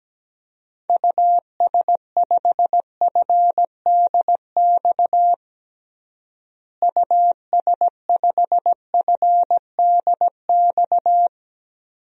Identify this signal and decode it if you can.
Morse code, heard as a keyed tone, about 17 wpm, 700 Hz. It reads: US5FDX US5FDX
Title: DXCC-100 Call Signs - 6 Characters - Repeated 2x - 17wpm